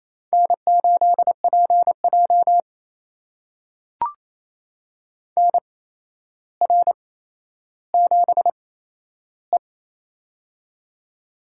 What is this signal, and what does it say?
Morse code, heard as a keyed tone, about 28 wpm, spaced as for 12 wpm, 700 Hz. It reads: N8PJ NR7E
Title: USA Call Signs - 4 Characters - 8x Character Spacing - 28wpm